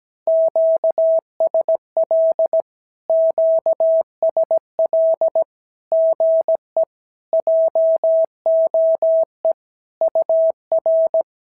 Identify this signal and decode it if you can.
Morse code, heard as a keyed tone, about 17 wpm, 650 Hz. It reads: QSL QSL GE JOE UR
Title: SOTA QSO Practice 17wpm